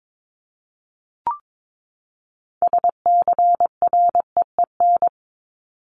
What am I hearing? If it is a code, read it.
Morse code, heard as a keyed tone, about 22 wpm, 700 Hz. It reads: SCREEN
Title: Top 751-1000 Words 22wpm